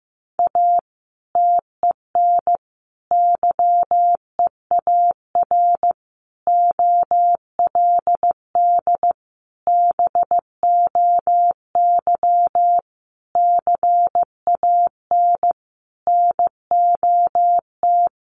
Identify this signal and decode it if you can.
Morse code, heard as a keyed tone, about 15 wpm, 700 Hz. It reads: A TEN YEAR OLD BOY CAN NOT
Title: Sentences from Top 1300 Words 15wpm